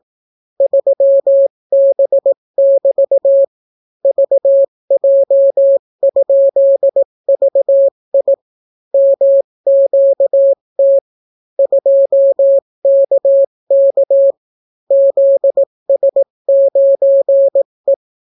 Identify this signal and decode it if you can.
Morse code, heard as a keyed tone, about 18 wpm, 550 Hz. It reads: 3B<BT> VJ?VI MQT 2KK ZS9E